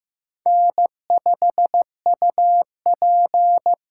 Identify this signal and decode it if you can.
Morse code, heard as a keyed tone, about 15 wpm, 700 Hz. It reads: N5UP
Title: USA Call Signs - 4 Characters - 4x Character Spacing - 15wpm